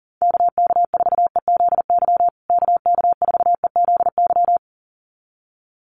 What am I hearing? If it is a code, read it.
Morse code, heard as a keyed tone, about 40 wpm, 700 Hz. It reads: KK4EZY KK4EZY
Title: USA Call Signs Repeated 2x 40wpm